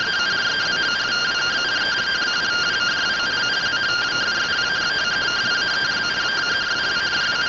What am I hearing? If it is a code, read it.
Two-tone teleprinter signal, about 170 Hz shift.
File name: findings / AXIAL fr130 3mm 409.gen-22 rtty rx